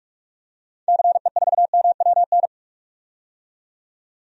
Morse code, heard as a keyed tone, about 46 wpm, 700 Hz. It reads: CE4MWN
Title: DXCC-100 Call Signs - 6 Characters - 45wpm